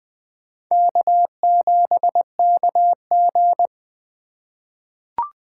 Morse code, heard as a keyed tone, about 20 wpm, 700 Hz. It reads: K7KG
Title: USA Call Signs - 4 Characters - 3x Character Spacing - 20wpm